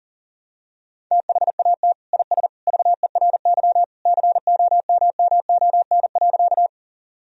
Morse code, heard as a keyed tone, about 40 wpm, 700 Hz. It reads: THAT IS VERY COMMON.
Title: Sentences from Top 1000 Words - Review Part 3 - 40wpm